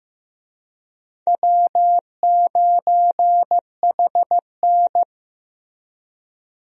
Morse code, heard as a keyed tone, about 15 wpm, 700 Hz. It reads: W9HN
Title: USA Call Signs - 4 Characters - Repeated 3x - 15wpm